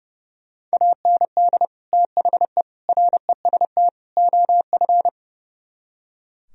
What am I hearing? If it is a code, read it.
Morse code, heard as a keyed tone, about 36 wpm, 700 Hz. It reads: AND THE REST OF
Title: Speed-Racing - Sets of 4 Words 30wpm